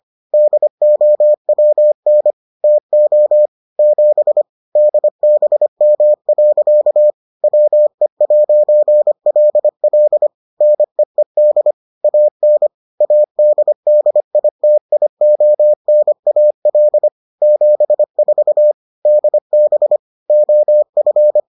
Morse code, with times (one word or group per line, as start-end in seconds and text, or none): 0.3-2.3 DOWN
2.6-3.5 TO
3.8-4.4 7
4.8-7.1 DBM.
7.4-10.3 WE'LL
10.6-11.7 NEED
12.0-12.7 AN
13.0-17.1 ADDITIONAL
17.4-18.7 74
19.1-20.0 DB
20.3-21.4 OF